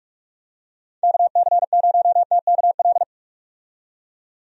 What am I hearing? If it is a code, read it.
Morse code, heard as a keyed tone, about 46 wpm, 700 Hz. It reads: KC0TKL
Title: USA Call Signs - 6 Characters - 45wpm